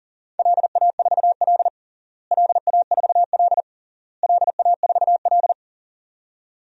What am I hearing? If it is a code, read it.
Morse code, heard as a keyed tone, about 40 wpm, 700 Hz. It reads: LA4L LA4L LA4L
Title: DXCC-100 Call Signs - 4 Characters - Repeated 3x - 40wpm